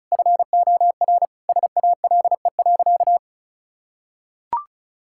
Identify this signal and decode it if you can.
Morse code, heard as a keyed tone, about 36 wpm, 700 Hz. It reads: FOR SALE.
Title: Sentences from Top 1600 Words 35wpm